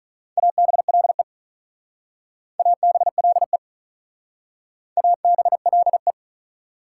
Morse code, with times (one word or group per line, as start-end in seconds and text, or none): 0.4-1.2 ABLE
2.6-3.6 ABLE
5.0-6.1 ABLE